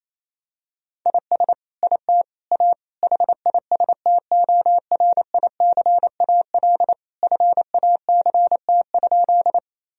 Morse code, heard as a keyed tone, about 28 wpm, 700 Hz. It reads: IS IT A HISTORICAL FACT?
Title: Sentences from Top 1000 Words - Review Part 4 - 28wpm